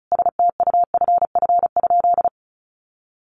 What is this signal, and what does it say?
Morse code, heard as a keyed tone, about 34 wpm, 700 Hz. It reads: STUFF?
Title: Sentences from Top 1400 Words 35wpm